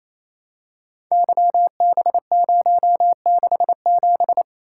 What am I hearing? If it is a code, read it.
Morse code, heard as a keyed tone, about 29 wpm, 700 Hz. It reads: YB067
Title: Code Groups - 5 Letter:Numbers - 28wpm